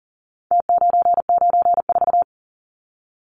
Morse code, heard as a keyed tone, about 40 wpm, 700 Hz. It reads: T994
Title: Numbers - Contest Style - 4-Digit 40wpm